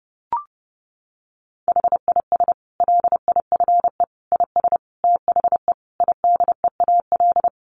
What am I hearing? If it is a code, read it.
Morse code, heard as a keyed tone, about 30 wpm, 700 Hz. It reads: HIS LIFE IS THE IDEAL
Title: Sentences from Top 1400 Words 30wpm